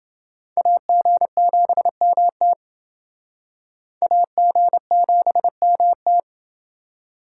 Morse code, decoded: AG7MT AG7MT

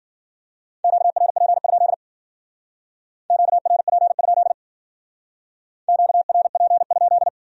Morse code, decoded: QRP? QRP? QRP?

